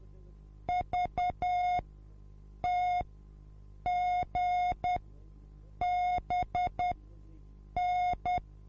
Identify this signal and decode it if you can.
Morse code, heard as a keyed tone, about 10 wpm, spaced as for 7 wpm, 720 Hz. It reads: VTGBN